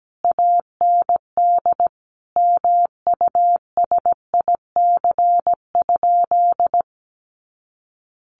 Morse code, heard as a keyed tone, about 17 wpm, 700 Hz. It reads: AND MUSIC?
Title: Sentences from Top 1200 Words 17wpm